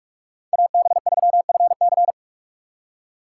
Morse code, heard as a keyed tone, about 54 wpm, 700 Hz. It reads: AB3FC
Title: Speed Racing - USA 2x2 Call Signs 45wpm